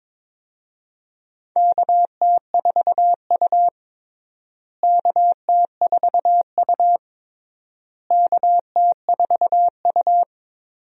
Morse code, heard as a keyed tone, about 22 wpm, 700 Hz. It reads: KT4U KT4U KT4U
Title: USA Call Signs - 4 Characters - Repeated 3x - 22wpm